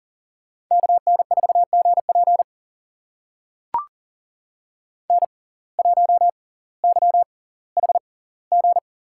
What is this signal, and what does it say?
Morse code, heard as a keyed tone, about 40 wpm, spaced as for 20 wpm, 700 Hz. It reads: KN4GP N1YHG
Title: USA Call Signs - 5 Characters - 6x Character Spacing - 40wpm